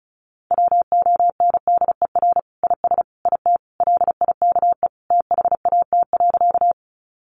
Morse code, heard as a keyed tone, about 34 wpm, 700 Hz. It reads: WONDER IS IT LIKE THAT.
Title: Sentences from Top 500 Words 35wpm